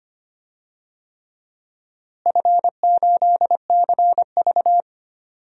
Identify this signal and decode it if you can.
Morse code, heard as a keyed tone, about 25 wpm, 700 Hz. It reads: F8CV
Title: DXCC-100 Call Signs - 4 Characters - Repeated 3x - 25wpm